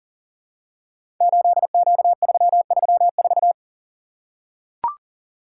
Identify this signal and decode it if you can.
Morse code, heard as a keyed tone, about 38 wpm, 700 Hz. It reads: 8Q334
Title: Code Groups - 5 Letter:Numbers - 40wpm